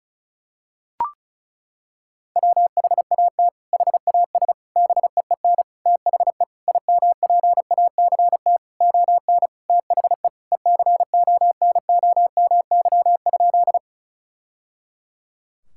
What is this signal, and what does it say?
Morse code, heard as a keyed tone, about 36 wpm, 700 Hz. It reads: WHAT HAS BEEN THE IMPACT ON THE ECONOMY?
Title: Sentences from Top 1000 Words - Review Part 6 - 35wpm